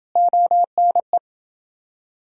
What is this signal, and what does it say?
Morse code, heard as a keyed tone, about 29 wpm, 700 Hz. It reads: ONE